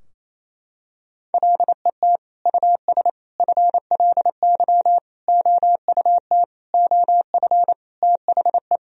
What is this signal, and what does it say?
Morse code, heard as a keyed tone, about 28 wpm, 700 Hz. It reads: LET US FLY OUT OF THE
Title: Sentences from Top 1000 Words - Review Part 2 - 28wpm